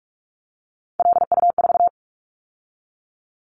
Morse code, heard as a keyed tone, about 46 wpm, 700 Hz. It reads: LU4
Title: Callsign Prefixes for Top 100 DXCC - 45wpm